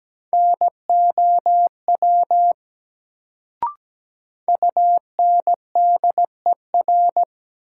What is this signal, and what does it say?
Morse code, heard as a keyed tone, about 17 wpm, 700 Hz. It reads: NOW UNDER
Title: Sets of 2 Words - New Word Drill - 17wpm